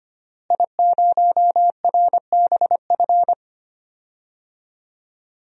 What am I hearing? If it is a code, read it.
Morse code, heard as a keyed tone, about 25 wpm, 700 Hz. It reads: I0RBF
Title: DXCC-100 Call Signs - 5 Characters - Repeated 2x - 25wpm